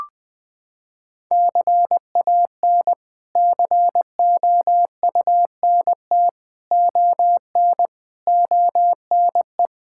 Morse code, heard as a keyed tone, about 21 wpm, 700 Hz. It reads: CAN COUNT ON ONE
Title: Sets of 8 Words - 20wpm